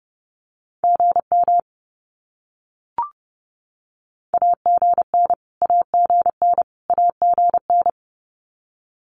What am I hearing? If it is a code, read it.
Morse code, heard as a keyed tone, about 30 wpm, 700 Hz. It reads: GM AGN AGN AGN
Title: Abbreviations Repeated 3x - 30wpm